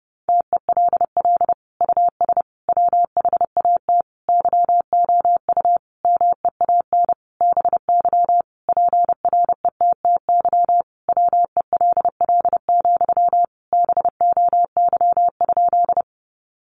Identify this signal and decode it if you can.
Morse code, heard as a keyed tone, about 30 wpm, 700 Hz. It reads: TELL US WHAT YOU MEAN BY PRETTY WELL, BOY?
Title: Sentences from Top 1000 Words 30wpm